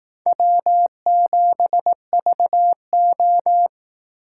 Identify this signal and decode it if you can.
Morse code, heard as a keyed tone, about 18 wpm, 700 Hz. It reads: W7VO
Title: Speed Racing - USA 1x2 Call Signs 15wpm